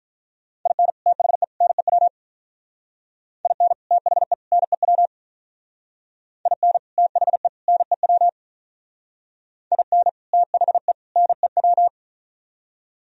Morse code, heard as a keyed tone, about 44 wpm, 700 Hz. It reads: IN THE NEW IN THE NEW IN THE NEW IN THE NEW
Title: Speed-Racing - Sets of 3 Words 35wpm